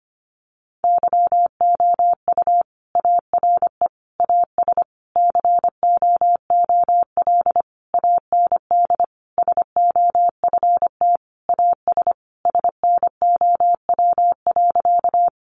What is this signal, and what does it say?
Morse code, heard as a keyed tone, about 25 wpm, 700 Hz. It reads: YOU ARE AS COOL AND SOFT AS SNOW.
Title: Sentences from Top 1500 Words 25wpm